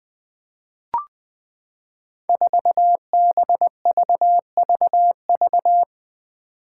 Morse code, heard as a keyed tone, about 20 wpm, 700 Hz. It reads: 4BVVV
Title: Five Challenging Letter-Number Combinations 20wpm